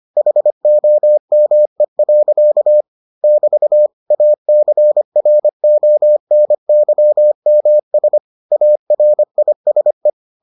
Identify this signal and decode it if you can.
Morse code, heard as a keyed tone, about 25 wpm, 600 Hz. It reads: HOME. <BT> ACRONYMS ARISE